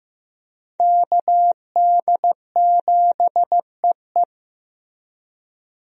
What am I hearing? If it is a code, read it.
Morse code, heard as a keyed tone, about 14 wpm, 700 Hz. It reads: KD7EE
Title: USA Call Signs - 5 Characters - Repeated 2x - 15wpm